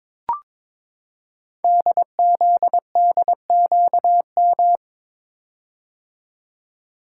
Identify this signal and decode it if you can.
Morse code, heard as a keyed tone, about 22 wpm, 700 Hz. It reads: DZDQM
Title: Code Groups - 5 Letters - 22wpm